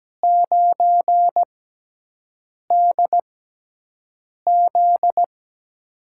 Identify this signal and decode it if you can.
Morse code, heard as a keyed tone, about 17 wpm, spaced as for 6 wpm, 700 Hz. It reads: 9DZ
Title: USA Call Signs - 4 Characters - 6x Character Spacing - 17wpm